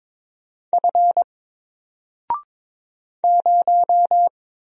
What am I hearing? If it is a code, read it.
Morse code, heard as a keyed tone, about 22 wpm, 700 Hz. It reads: F 0
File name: Single-Letter-Number-22wpm